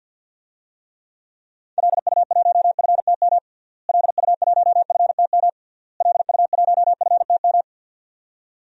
Morse code, heard as a keyed tone, about 51 wpm, 700 Hz. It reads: LU1FTM LU1FTM LU1FTM
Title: DXCC-100 Call Signs - 6 Characters - Repeated 3x - 50wpm